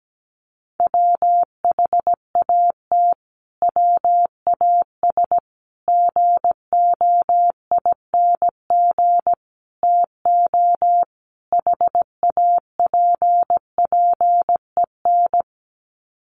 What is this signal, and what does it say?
Morse code, heard as a keyed tone, about 17 wpm, 700 Hz. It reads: WHAT WAS GOING TO HAPPEN
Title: Sets of 5 Words Encore 17wpm